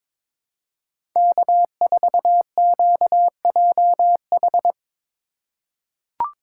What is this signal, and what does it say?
Morse code, heard as a keyed tone, about 22 wpm, 700 Hz. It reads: K4QJH
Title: USA Call Signs - 2x Character Spacing - 22wpm